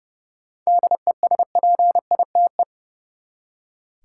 Morse code, heard as a keyed tone, about 30 wpm, 700 Hz. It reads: DESPITE
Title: Seven Letter Words - 30wpm